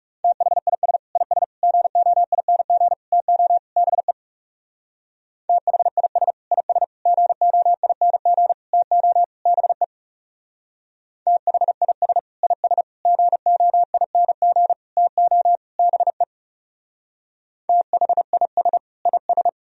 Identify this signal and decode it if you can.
Morse code, heard as a keyed tone, about 37 wpm, 700 Hz. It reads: THIS IS GOING TO BE THIS IS GOING TO BE THIS IS GOING TO BE THIS IS
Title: Speed-Racing - Sets of 5 Words 30wpm